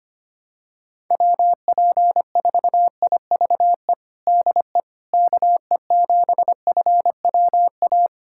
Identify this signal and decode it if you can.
Morse code, heard as a keyed tone, about 25 wpm, 700 Hz. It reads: WP4IVE DE KE7FWA
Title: USA Callsign DE Callsign 25wpm